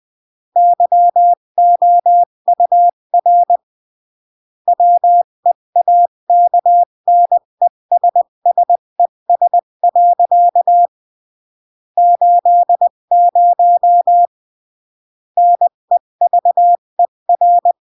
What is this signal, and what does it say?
Morse code, heard as a keyed tone, about 20 wpm, 700 Hz. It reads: YOUR WEAKNESSES. 80 NEVER
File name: CW / jodorowsky / 82 commandments 20-15